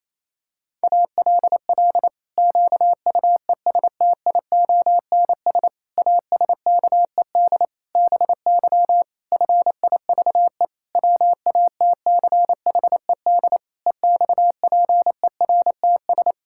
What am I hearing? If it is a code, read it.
Morse code, heard as a keyed tone, about 28 wpm, 700 Hz. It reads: ALL QUESTIONS ASKED BY FIVE WATCHED EXPERTS
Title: Pangram Challenge v2 - 28wpm